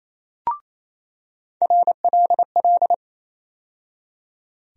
Morse code, heard as a keyed tone, about 28 wpm, 700 Hz. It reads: RLL
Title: USA Call Sign Postfixes - 3 Letters 28wpm